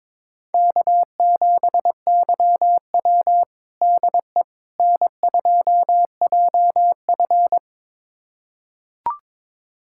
Morse code, heard as a keyed tone, about 23 wpm, 700 Hz. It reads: K7YW DE N2JF
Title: USA Callsign DE Callsign - 4 Characters - 22wpm